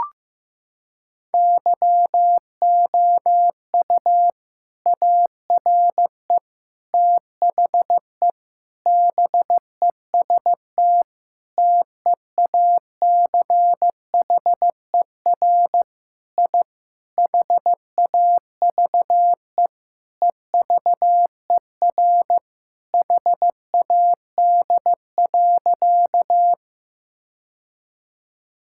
Morse code, heard as a keyed tone, about 15 wpm, 700 Hz. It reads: YOU ARE THE BEST TEACHER I HAVE EVER HAD.
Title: Sentences from Top 1000 Words - Review Part 6 - 15wpm